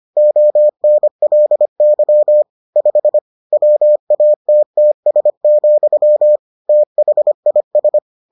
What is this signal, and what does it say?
Morse code, heard as a keyed tone, about 25 wpm, 600 Hz. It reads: ONLY 5 WATTS, THIS